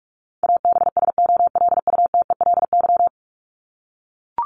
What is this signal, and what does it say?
Morse code, heard as a keyed tone, about 44 wpm, 700 Hz. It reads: ABSOLUTELY
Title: LY Suffix - Top 100 Words with Base Words 45wpm